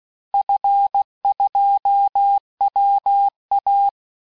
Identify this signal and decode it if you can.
Morse code, heard as a keyed tone, about 16 wpm, 800 Hz. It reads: F2WA
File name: cours-du-lundi files / cours-1580-fin-QSO